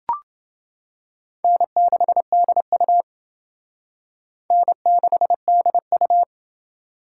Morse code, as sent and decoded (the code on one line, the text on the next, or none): -. -.... -.. ..- / -. -.... -.. ..-
N6DU N6DU